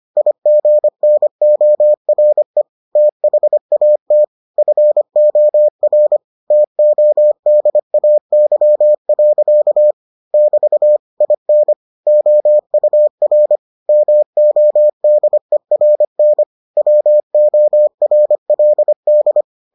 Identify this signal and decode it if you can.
Morse code, heard as a keyed tone, about 25 wpm, 600 Hz. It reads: IGNORE THAT FOR TODAY. <BT> IN OUR MODERN WORLD